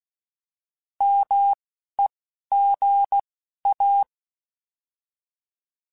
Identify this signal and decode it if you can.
Morse code, heard as a keyed tone, about 16 wpm, 800 Hz. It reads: M E G A
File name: Alphabetical / Alphabetical mp3s / Ms @10